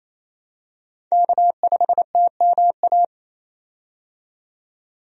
Morse code, heard as a keyed tone, about 27 wpm, 700 Hz. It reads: K5TMA
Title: USA Call Signs - 5 Characters - 28wpm